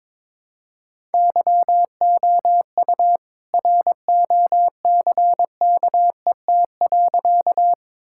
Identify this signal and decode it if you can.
Morse code, heard as a keyed tone, about 22 wpm, 700 Hz. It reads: YOU ROCKET.
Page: morse-code-ninja-practice-22wpm